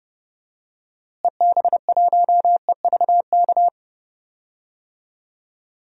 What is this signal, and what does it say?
Morse code, heard as a keyed tone, about 30 wpm, 700 Hz. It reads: EB1EVK
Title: DXCC-100 Call Signs - 6 Characters - Repeated 2x - 30wpm